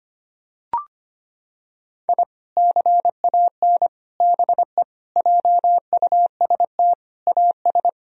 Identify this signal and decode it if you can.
Morse code, heard as a keyed tone, about 25 wpm, 700 Hz. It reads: I CAN BE JUST AS
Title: Sentences from Top 1000 Words - Review Part 3 - 25wpm